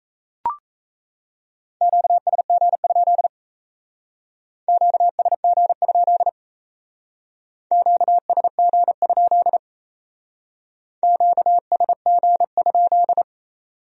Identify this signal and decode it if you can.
Morse code, heard as a keyed tone, about 35 wpm, 700 Hz. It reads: QSG? QSG? QSG? QSG?
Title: Speed-Racing Q-Codes - 28wpm